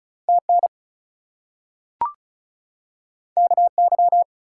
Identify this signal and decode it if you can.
Morse code, heard as a keyed tone, about 36 wpm, 700 Hz. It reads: TN KY
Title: US States and Canadian Provinces 35wpm